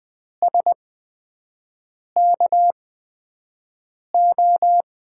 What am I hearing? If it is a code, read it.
Morse code, heard as a keyed tone, about 20 wpm, spaced as for 5 wpm, 700 Hz. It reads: SKO
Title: USA Call Signs - 6 Characters - 8x Character Spacing - 20wpm